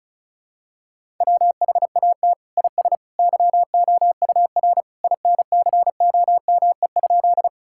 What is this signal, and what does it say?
Morse code, heard as a keyed tone, about 34 wpm, 700 Hz. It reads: WHAT IS YOUR INCOME?
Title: Sentences from Top 1000 Words - Review Part 5 - 35wpm